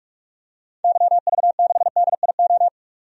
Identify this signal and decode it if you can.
Morse code, heard as a keyed tone, about 46 wpm, 700 Hz. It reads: YV6DIO
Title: DXCC-100 Call Signs - 6 Characters - 45wpm